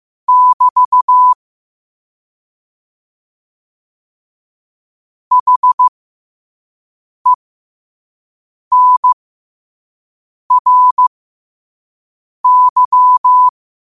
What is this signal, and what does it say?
Morse code, heard as a keyed tone, about 15 wpm, spaced as for 5 wpm, 1000 Hz. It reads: <BT> HENRY
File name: qotdfiles / 5wpm